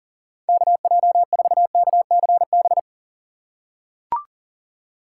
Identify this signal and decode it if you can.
Morse code, heard as a keyed tone, about 40 wpm, 700 Hz. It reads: KJ4KCB
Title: USA Call Signs - 6 Characters - 8x Character Spacing - 40wpm